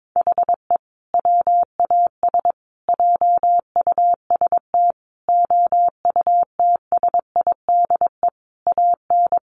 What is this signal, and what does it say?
Morse code, heard as a keyed tone, about 22 wpm, 700 Hz. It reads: HE WAS JUST OUTSIDE AN
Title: Sentences from Top 1100 Words 22wpm